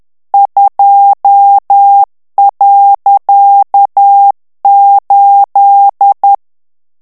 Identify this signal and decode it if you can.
Morse code, heard as a keyed tone, about 10 wpm, 800 Hz. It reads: 2.8
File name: ANGLAIS texte clair 3 vitesse 10 mots minute